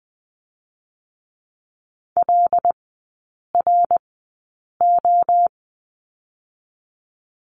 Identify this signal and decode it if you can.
Morse code, heard as a keyed tone, about 20 wpm, spaced as for 8 wpm, 700 Hz. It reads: LRO
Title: 087 Lesson 15e - Number 5 - Three Characters 20wpm